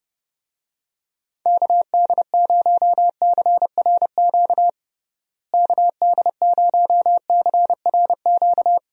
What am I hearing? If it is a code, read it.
Morse code, heard as a keyed tone, about 32 wpm, 700 Hz. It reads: KD0CRQ KD0CRQ
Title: USA Call Signs - 6 Characters - Repeated 3x - 30wpm